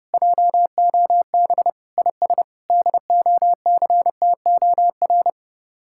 Morse code, heard as a keyed tone, about 33 wpm, 700 Hz. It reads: JOB IS DOCTOR